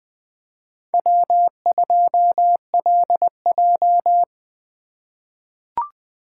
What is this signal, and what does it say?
Morse code, heard as a keyed tone, about 20 wpm, 700 Hz. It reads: W2LJ
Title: USA Call Signs - 4 Characters - 8x Character Spacing - 20wpm